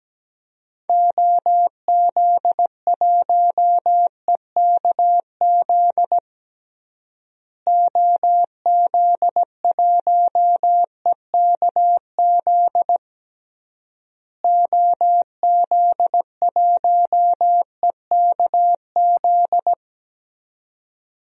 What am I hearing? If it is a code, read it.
Morse code, heard as a keyed tone, about 17 wpm, 700 Hz. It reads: OZ1EKZ OZ1EKZ OZ1EKZ
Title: DXCC-100 Call Signs - 6 Characters - Repeated 3x - 17wpm